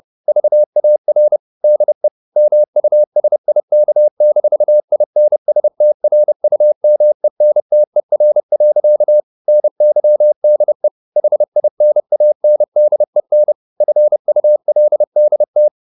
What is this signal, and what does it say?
Morse code, heard as a keyed tone, about 30 wpm, 600 Hz. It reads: VAR DE MUSIK-INSTRUMENTER. NYDE HINANDEN FULDT
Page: Ferie 30 wpm